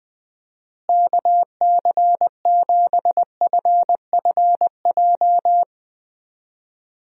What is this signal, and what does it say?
Morse code, heard as a keyed tone, about 20 wpm, 700 Hz. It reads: KC7FFJ
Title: USA Call Signs - 6 Characters - Repeated 2x - 20wpm